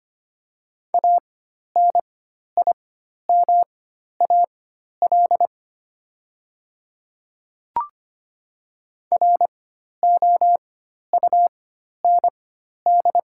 Morse code, heard as a keyed tone, about 25 wpm, spaced as for 12 wpm, 700 Hz. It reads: ANIMAL ROUND
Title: 4x Character Spacing - Top 200 Words - 25wpm